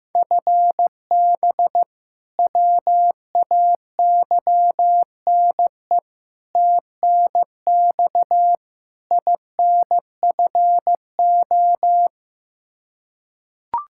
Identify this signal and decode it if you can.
Morse code, heard as a keyed tone, about 15 wpm, 700 Hz. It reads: FB WAYNE TNX INFO
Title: K1USN SST QSO Practice 15wpm